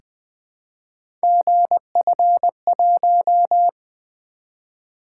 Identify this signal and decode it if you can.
Morse code, heard as a keyed tone, about 21 wpm, 700 Hz. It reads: GF1
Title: Three Letter-Number Combinations - 20wpm